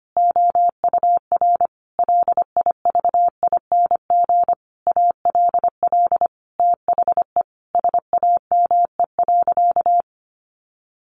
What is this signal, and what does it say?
Morse code, heard as a keyed tone, about 25 wpm, 700 Hz. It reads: OUR LIVING ALL THE SAME.
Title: Sentences from Top 1500 Words 25wpm